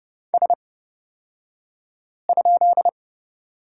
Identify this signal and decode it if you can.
Morse code, heard as a keyed tone, about 30 wpm, spaced as for 5 wpm, 700 Hz. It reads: S?
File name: Lesson 38d - Letter q - Two Characters 30wpm